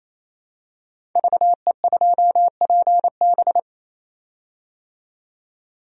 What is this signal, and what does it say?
Morse code, heard as a keyed tone, about 28 wpm, 700 Hz. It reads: VE2PB
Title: DXCC-100 Call Signs - 5 Characters - 28wpm